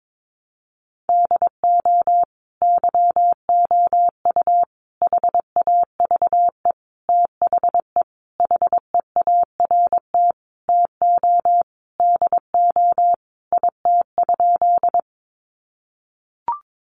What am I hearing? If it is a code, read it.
Morse code, heard as a keyed tone, about 22 wpm, 700 Hz. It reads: DO YOU HAVE THE HEART TO DO IT?
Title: Sentences from Top 1200 Words 22wpm